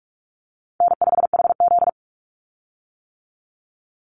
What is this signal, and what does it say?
Morse code, heard as a keyed tone, about 44 wpm, 700 Hz. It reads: N5HZ
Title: Call Signs - 45wpm